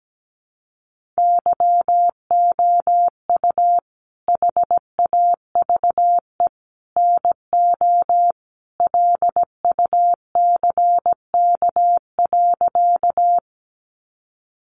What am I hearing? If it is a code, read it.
Morse code, heard as a keyed tone, about 17 wpm, 700 Hz. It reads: YOU HAVE NO LUCK.
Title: Sentences from Top 1500 Words 17wpm